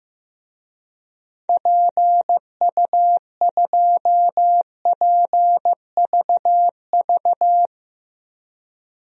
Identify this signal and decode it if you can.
Morse code, heard as a keyed tone, about 15 wpm, 700 Hz. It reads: PU2PVV
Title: DXCC-100 Call Signs - 6 Characters - 15wpm